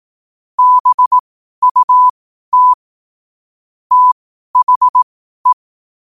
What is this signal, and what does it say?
Morse code, heard as a keyed tone, about 18 wpm, spaced as for 12 wpm, 1000 Hz. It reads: BUT THE
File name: qotdfiles / 12wpm